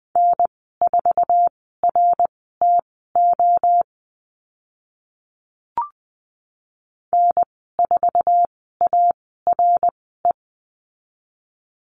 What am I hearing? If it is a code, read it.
Morse code, heard as a keyed tone, about 20 wpm, spaced as for 14 wpm, 700 Hz. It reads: N4RTO N4ARE
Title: 069 Lesson 12g - Letter d - Call Signs 20wpm